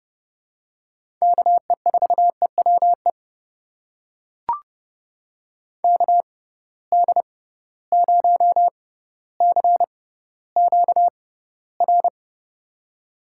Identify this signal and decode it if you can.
Morse code, heard as a keyed tone, about 30 wpm, spaced as for 15 wpm, 700 Hz. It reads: KE4EWE KD0CQR
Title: USA Call Signs - 6 Characters - 6x Character Spacing - 30wpm